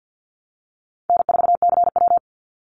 Morse code, heard as a keyed tone, about 50 wpm, 700 Hz. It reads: N4CW
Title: Easy Call Signs - 50wpm